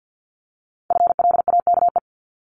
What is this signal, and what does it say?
Morse code, heard as a keyed tone, about 50 wpm, 700 Hz. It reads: FLAKE
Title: Multi-letter Phonemes 50wpm